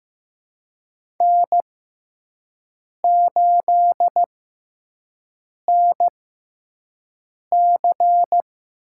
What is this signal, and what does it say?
Morse code, heard as a keyed tone, about 15 wpm, spaced as for 5 wpm, 700 Hz. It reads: N8NC
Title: USA Call Signs - 5 Characters - 6x Character Spacing - 15wpm